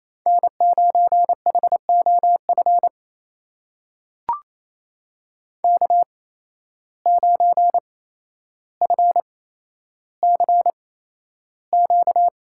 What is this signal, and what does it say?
Morse code, heard as a keyed tone, about 28 wpm, spaced as for 12 wpm, 700 Hz. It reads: N9HOF K9FCQ
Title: USA Call Signs - 5 Characters - 8x Character Spacing - 28wpm